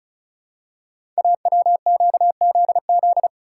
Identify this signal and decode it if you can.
Morse code, heard as a keyed tone, about 36 wpm, 700 Hz. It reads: AWQZZ
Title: Code Groups - 5 Letters - 35wpm